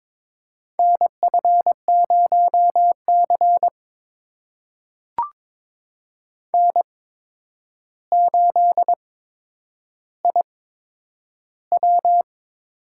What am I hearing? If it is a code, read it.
Morse code, heard as a keyed tone, about 22 wpm, spaced as for 9 wpm, 700 Hz. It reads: NF0C N8IW
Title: USA Call Signs - 4 Characters - 8x Character Spacing - 22wpm